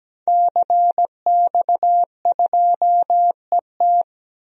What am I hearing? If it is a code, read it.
Morse code, heard as a keyed tone, about 17 wpm, 700 Hz. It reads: CX2ET
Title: DXCC-100 Call Signs - 5 Characters - Repeated 3x - 17wpm